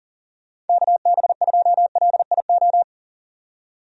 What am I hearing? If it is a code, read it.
Morse code, heard as a keyed tone, about 40 wpm, 700 Hz. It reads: KB2LIO
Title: USA Call Signs - 6 Characters - 40wpm